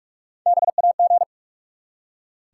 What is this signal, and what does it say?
Morse code, heard as a keyed tone, about 44 wpm, 700 Hz. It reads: BAG
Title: Three Letter Words - 45wpm